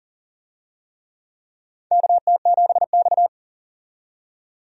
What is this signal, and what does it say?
Morse code, heard as a keyed tone, about 40 wpm, 700 Hz. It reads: KT7X
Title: USA Call Signs - 4 Characters - 40wpm